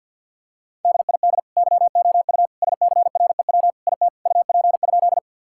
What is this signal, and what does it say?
Morse code, heard as a keyed tone, about 50 wpm, 700 Hz. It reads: DID YOU SCREW IT UP?
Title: Sentences from Top 1600 Words 50wpm